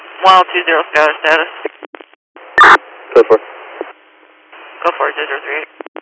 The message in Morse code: none

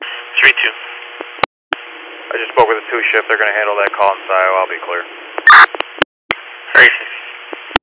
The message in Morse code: none

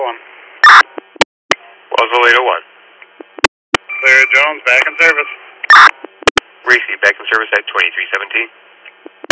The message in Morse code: none

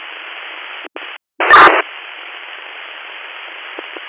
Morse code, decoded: none

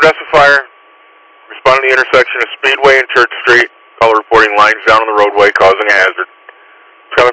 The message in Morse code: none

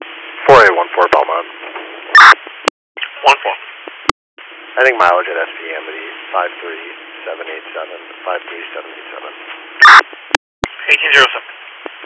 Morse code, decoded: none

